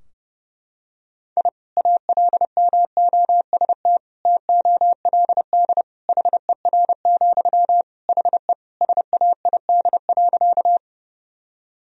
Morse code, heard as a keyed tone, about 32 wpm, 700 Hz. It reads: I ALMOST TOLD HER, HE SAID.
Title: Sentences from Top 1000 Words - Review Part 4 - 30wpm